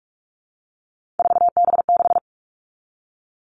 Morse code, heard as a keyed tone, about 44 wpm, 700 Hz. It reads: 4B6
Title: All Cut Numbers - 3-Digit 45wpm